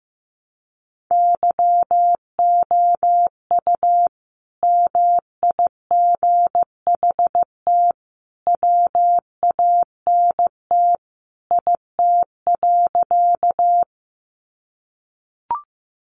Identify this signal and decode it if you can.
Morse code, heard as a keyed tone, about 15 wpm, 700 Hz. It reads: YOU MIGHT WANT IT.
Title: Sentences from Top 300 Words 15wpm